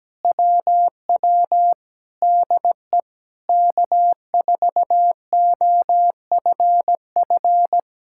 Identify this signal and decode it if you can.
Morse code, heard as a keyed tone, about 17 wpm, 700 Hz. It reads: WW DE K4OFF
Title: Speed-Racing Common QSO Phrases - 17wpm